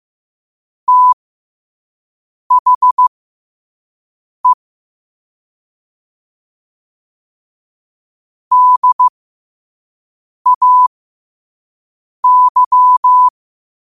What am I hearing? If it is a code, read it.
Morse code, heard as a keyed tone, about 15 wpm, spaced as for 5 wpm, 1000 Hz. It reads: THE DAY
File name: qotdfiles / 5wpm